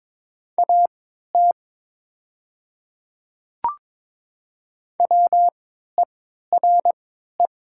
Morse code, heard as a keyed tone, about 22 wpm, spaced as for 13 wpm, 700 Hz. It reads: AT WERE
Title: 3x Character Spacing - Top 100 Words - 22wpm